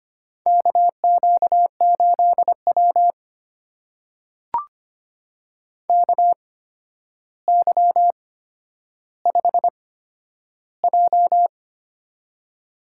Morse code, heard as a keyed tone, about 25 wpm, spaced as for 11 wpm, 700 Hz. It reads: KQ8W KY5J
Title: USA Call Signs - 4 Characters - 8x Character Spacing - 25wpm